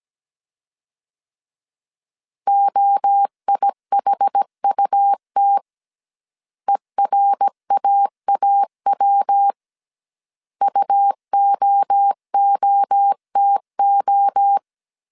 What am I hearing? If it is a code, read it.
Morse code, heard as a keyed tone, about 17 wpm, 800 Hz. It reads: OIHUT ERAAW UOOTO